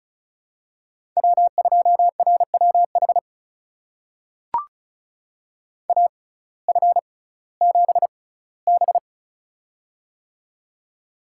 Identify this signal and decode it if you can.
Morse code, heard as a keyed tone, about 36 wpm, spaced as for 19 wpm, 700 Hz. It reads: W2RWH AF7B
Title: USA Call Signs - 6x Character Spacing - 35wpm